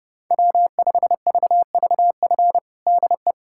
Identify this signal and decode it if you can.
Morse code, heard as a keyed tone, about 32 wpm, 700 Hz. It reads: W5VVF DE
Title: USA Callsign DE Callsign - 5 Characters - 30wpm